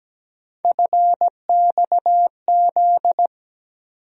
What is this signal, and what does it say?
Morse code, heard as a keyed tone, about 17 wpm, 700 Hz. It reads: FXZ
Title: USA Call Sign Postfixes - 3 Letters 17wpm